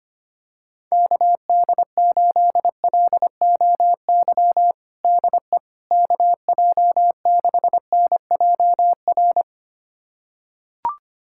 Morse code, heard as a keyed tone, about 25 wpm, 700 Hz. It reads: KD8LOY DE KJ6NJR
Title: USA Callsign DE Callsign - 6 Characters - 25wpm